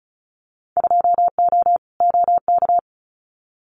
Morse code, heard as a keyed tone, about 34 wpm, 700 Hz. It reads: 2O OK
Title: Winter Field Day Exchanges 35wpm